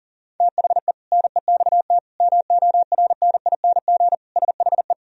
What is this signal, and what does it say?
Morse code, heard as a keyed tone, about 40 wpm, 700 Hz. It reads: THE NEXT MORNING SHE
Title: Sentences from Top 1000 Words - Review Part 2 - 40wpm